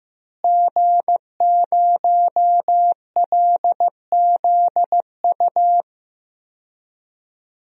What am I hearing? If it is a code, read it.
Morse code, heard as a keyed tone, about 16 wpm, 700 Hz. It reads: G0LZU
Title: DXCC-100 Call Signs - 5 Characters - Repeated 3x - 15wpm